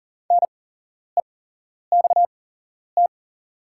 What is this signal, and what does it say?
Morse code, heard as a keyed tone, about 40 wpm, spaced as for 11 wpm, 700 Hz. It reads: NEXT
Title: 8x Character Spacing - Top 500 Words - 40wpm